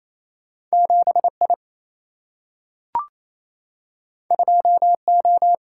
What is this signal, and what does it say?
Morse code, heard as a keyed tone, about 28 wpm, 700 Hz. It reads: 7 I 2 O
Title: Single-Letter Single-Number 28wpm